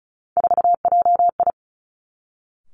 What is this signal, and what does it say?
Morse code, heard as a keyed tone, about 36 wpm, 700 Hz. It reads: 4JI
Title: Three Challenging Letter-Number Combinations 35wpm